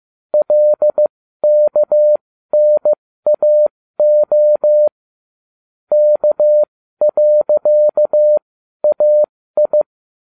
Morse code, heard as a keyed tone, about 15 wpm, spaced as for 12 wpm, 600 Hz. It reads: LKNAO K.AI